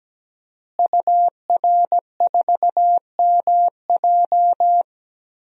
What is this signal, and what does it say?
Morse code, heard as a keyed tone, about 17 wpm, 700 Hz. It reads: UR4MJ
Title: DXCC-100 Call Signs - 5 Characters - Repeated 2x - 17wpm